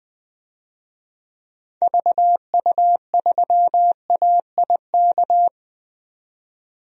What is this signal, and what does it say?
Morse code, heard as a keyed tone, about 20 wpm, 700 Hz. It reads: VU3AIK
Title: DXCC-100 Call Signs - 6 Characters - 20wpm